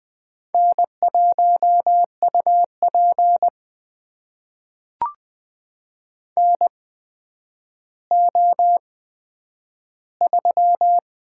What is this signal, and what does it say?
Morse code, heard as a keyed tone, about 20 wpm, spaced as for 10 wpm, 700 Hz. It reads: N1UP NO3
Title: USA Call Signs - 4 Characters - 8x Character Spacing - 20wpm